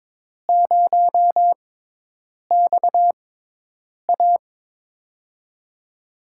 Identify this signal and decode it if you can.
Morse code, heard as a keyed tone, about 22 wpm, spaced as for 8 wpm, 700 Hz. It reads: 0XA
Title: USA Call Signs - 6x Character Spacing - 22wpm